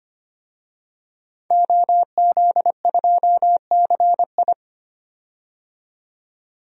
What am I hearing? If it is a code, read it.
Morse code, heard as a keyed tone, about 25 wpm, 700 Hz. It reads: OZ2CI
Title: DXCC-100 Call Signs - 5 Characters - 25wpm